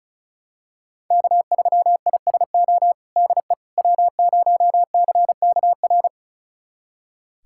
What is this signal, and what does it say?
Morse code, heard as a keyed tone, about 34 wpm, 700 Hz. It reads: K3ISO DE W0CKR
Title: USA Callsign DE Callsign - 5 Characters - 35wpm